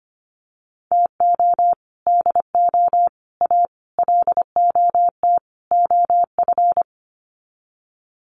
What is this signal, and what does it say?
Morse code, heard as a keyed tone, about 25 wpm, 700 Hz. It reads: TO DO A LOT OF
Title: Sets of 5 Words Encore 25wpm